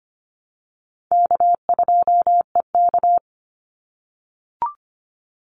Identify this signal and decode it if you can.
Morse code, heard as a keyed tone, about 25 wpm, 700 Hz. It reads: K2EK
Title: Easy Call Signs - 25wpm